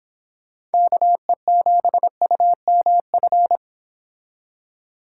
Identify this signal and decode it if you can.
Morse code, heard as a keyed tone, about 28 wpm, 700 Hz. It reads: KE7UMF